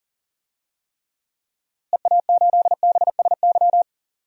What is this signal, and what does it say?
Morse code, heard as a keyed tone, about 40 wpm, 700 Hz. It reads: EA8BSY
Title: DXCC-100 Call Signs - 6 Characters - Repeated 2x - 40wpm